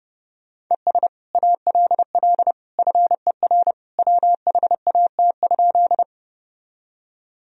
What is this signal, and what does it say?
Morse code, heard as a keyed tone, about 30 wpm, 700 Hz. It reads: ES ALL FER WHAT?
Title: Sentences from Top 100 Words and Abbreviations 30wpm